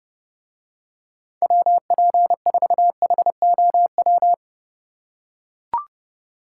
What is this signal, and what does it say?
Morse code, heard as a keyed tone, about 28 wpm, 700 Hz. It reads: WP4HOW